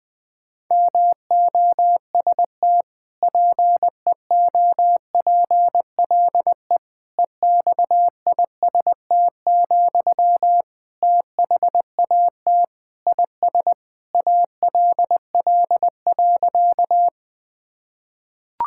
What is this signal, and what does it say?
Morse code, heard as a keyed tone, about 21 wpm, 700 Hz. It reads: MOST PEOPLE EXIST, THAT IS ALL.
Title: Sentences from Top 1000 Words - Review Part 5 - 20wpm